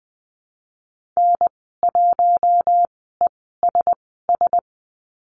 Morse code, heard as a keyed tone, about 20 wpm, spaced as for 15 wpm, 700 Hz. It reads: N1ESS
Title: 042 Lesson 8g - Number 1 - Call Signs 20wpm